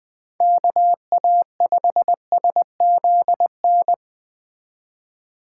Morse code, heard as a keyed tone, about 20 wpm, 700 Hz. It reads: KA5SZN
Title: Speed Racing - USA 2x3 Call Signs 17wpm